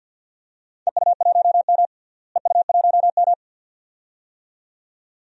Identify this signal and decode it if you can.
Morse code, heard as a keyed tone, about 52 wpm, 700 Hz. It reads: EU1M EU1M
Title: DXCC-100 Call Signs - 4 Characters - Repeated 2x - 50wpm